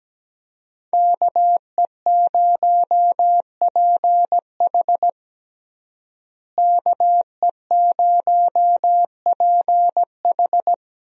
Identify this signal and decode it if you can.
Morse code, heard as a keyed tone, about 17 wpm, 700 Hz. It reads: KE0PH KE0PH
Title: USA Call Signs - 5 Characters - Repeated 3x - 17wpm